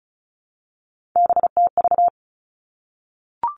Morse code, heard as a keyed tone, about 36 wpm, 700 Hz. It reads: BTV
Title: Three Challenging Letter-Number Combinations 35wpm